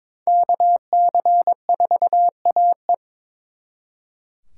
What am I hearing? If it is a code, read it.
Morse code, heard as a keyed tone, about 26 wpm, 700 Hz. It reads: KC4AE
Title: Speed Racing - USA Call Signs 22wpm